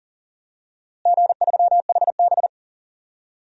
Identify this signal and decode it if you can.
Morse code, heard as a keyed tone, about 40 wpm, 700 Hz. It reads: G3HB
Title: DXCC-100 Call Signs - 4 Characters - Repeated 2x - 40wpm